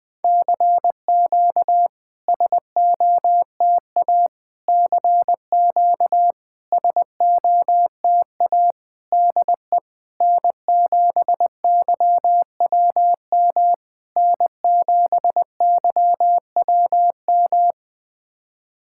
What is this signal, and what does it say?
Morse code, heard as a keyed tone, about 20 wpm, 700 Hz. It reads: CQ SOTA CQ SOTA DE N7YWM N7YWM
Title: SOTA QSO Practice 20wpm